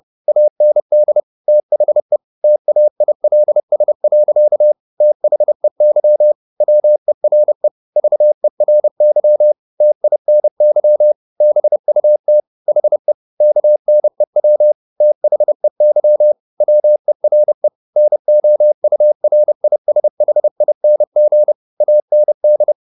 Morse code, heard as a keyed tone, about 30 wpm, 600 Hz. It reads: AND THE TAILS. THEY WERE VERY TINY BUT HE KNEW THEY WERE NOURISHING AND